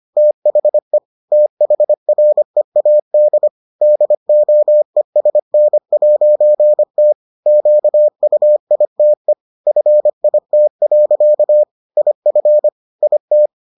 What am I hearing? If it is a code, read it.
Morse code, heard as a keyed tone, about 25 wpm, 600 Hz. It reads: THE THREAD DOESN'T QUITE FIT. IF IT